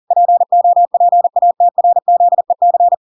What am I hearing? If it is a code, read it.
Morse code, heard as a keyed tone, about 40 wpm, 700 Hz. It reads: POPATRZEC